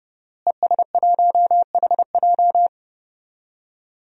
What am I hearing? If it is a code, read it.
Morse code, heard as a keyed tone, about 30 wpm, 700 Hz. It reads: ES1HJ
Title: DXCC-100 Call Signs - 5 Characters - Repeated 3x - 30wpm